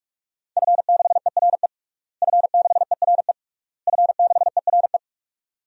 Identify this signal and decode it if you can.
Morse code, heard as a keyed tone, about 46 wpm, 700 Hz. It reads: F6ERE F6ERE F6ERE
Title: DXCC-100 Call Signs - 5 Characters - Repeated 3x - 45wpm